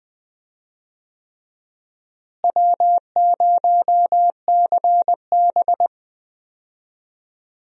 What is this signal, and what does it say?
Morse code, heard as a keyed tone, about 20 wpm, 700 Hz. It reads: W0CB
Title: USA Call Signs - 4 Characters - Repeated 3x - 20wpm